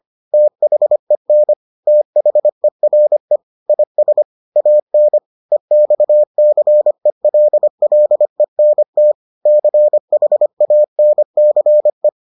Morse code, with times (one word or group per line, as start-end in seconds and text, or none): 0.3-1.5 THEN
1.9-3.4 THERE
3.7-4.2 IS
4.6-5.2 AN
5.5-9.1 EXCELLENT
9.5-12.1 CHANCE